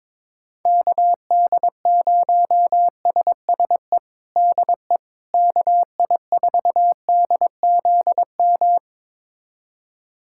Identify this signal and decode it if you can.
Morse code, heard as a keyed tone, about 22 wpm, 700 Hz. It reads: KD0SSE DE KI4DZM
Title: USA Callsign DE Callsign 22wpm